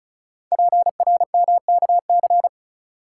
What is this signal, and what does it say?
Morse code, heard as a keyed tone, about 35 wpm, 700 Hz. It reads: PRMKC
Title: Code Groups - 5 Letters - 35wpm